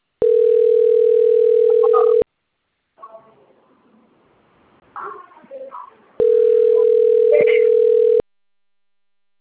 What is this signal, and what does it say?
Ringback tone (North America), 440 + 480 Hz.